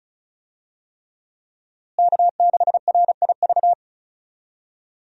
Morse code, heard as a keyed tone, about 36 wpm, 700 Hz. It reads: K6RIV
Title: USA Call Signs - 5 Characters - Repeated 3x - 35wpm